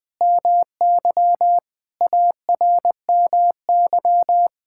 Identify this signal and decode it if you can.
Morse code, heard as a keyed tone, about 20 wpm, 700 Hz. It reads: MY ARMY
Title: Sentences from Top 1000 Words - Review Part 5 - 20wpm